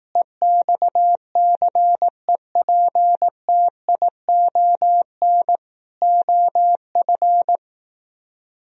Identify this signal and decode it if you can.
Morse code, heard as a keyed tone, about 18 wpm, 700 Hz. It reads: EXCEPTION OF
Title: Speed-Racing - Sets of 4 Words 15wpm